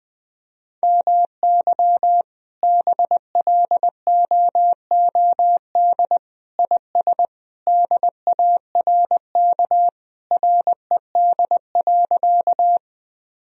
Morse code, heard as a keyed tone, about 19 wpm, 700 Hz. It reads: MY BLOOD IS DARK RED.
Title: Sentences from Top 1000 Words - Review Part 2 - 20wpm